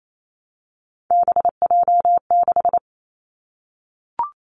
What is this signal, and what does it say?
Morse code, heard as a keyed tone, about 28 wpm, 700 Hz. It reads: BJ6
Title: Three Challenging Letter-Number Combinations 28wpm